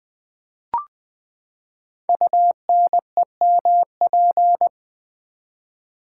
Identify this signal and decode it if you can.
Morse code, heard as a keyed tone, about 20 wpm, 700 Hz. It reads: UNEMP
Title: Build A Word - 5 Letters At A Time - 12-Letter Words - 20wpm